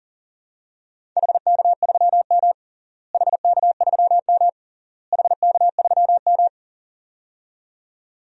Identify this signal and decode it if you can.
Morse code, heard as a keyed tone, about 40 wpm, 700 Hz. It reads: HK3M HK3M HK3M
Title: DXCC-100 Call Signs - 4 Characters - Repeated 3x - 40wpm